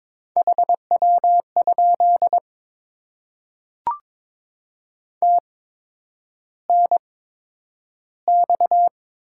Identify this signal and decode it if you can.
Morse code, heard as a keyed tone, about 22 wpm, spaced as for 9 wpm, 700 Hz. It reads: HW? TNX
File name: Top 20 QSO Elements - 8x Character Spacing - 22wpm